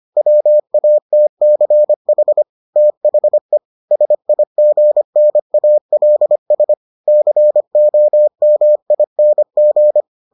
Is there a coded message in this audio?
Yes: Morse code from 0.2 to 10.0 s, heard as a keyed tone, about 25 wpm, 600 Hz. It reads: WATCH THE SIGNALS COMING